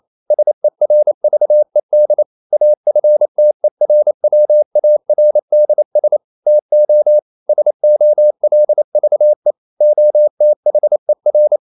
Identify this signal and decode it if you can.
Morse code, heard as a keyed tone, about 28 wpm, 600 Hz. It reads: SERVED AFTERWARDS TO SOLVE OTHER
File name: CW / Practice / EnglishQuotations0128